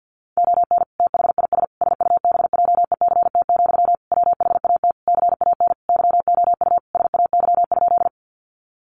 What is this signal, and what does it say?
Morse code, heard as a keyed tone, about 50 wpm, 700 Hz. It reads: ON THIS SUBJECT, WHAT CAN YOU SAY?
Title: Sentences from Top 800 Words 50wpm